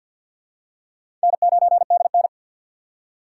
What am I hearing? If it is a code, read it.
Morse code, heard as a keyed tone, about 52 wpm, 700 Hz. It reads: N9DN
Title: USA Call Signs - 4 Characters - 6x Character Spacing - 50wpm